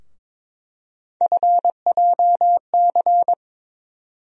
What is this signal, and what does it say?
Morse code, heard as a keyed tone, about 22 wpm, 700 Hz. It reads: FJC
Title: USA Call Sign Postfixes - 3 Letters 22wpm